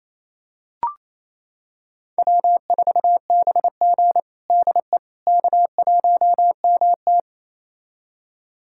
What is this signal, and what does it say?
Morse code, heard as a keyed tone, about 28 wpm, 700 Hz. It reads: W4BG DE K1MT
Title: USA Callsign DE Callsign - 4 Characters - 28wpm